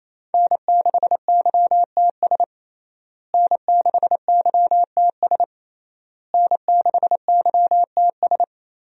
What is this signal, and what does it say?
Morse code, heard as a keyed tone, about 28 wpm, 700 Hz. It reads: N6YTS N6YTS N6YTS
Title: USA Call Signs - 5 Characters - Repeated 3x - 28wpm